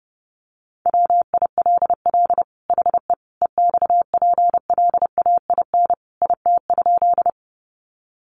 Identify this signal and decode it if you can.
Morse code, heard as a keyed tone, about 30 wpm, 700 Hz. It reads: WILL HE EXPLAIN IT?
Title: Sentences from Top 1100 Words 30wpm